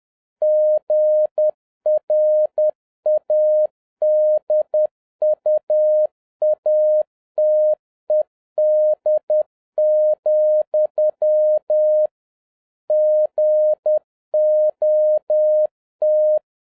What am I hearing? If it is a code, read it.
Morse code, heard as a keyed tone, about 10 wpm, 610 Hz. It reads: GRADUATED, GOT